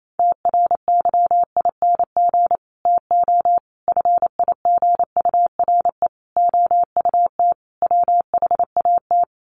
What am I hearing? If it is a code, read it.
Morse code, heard as a keyed tone, about 28 wpm, 700 Hz. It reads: TRYING TO FIGURE OUT WHAT
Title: Sets of 5 Words 28wpm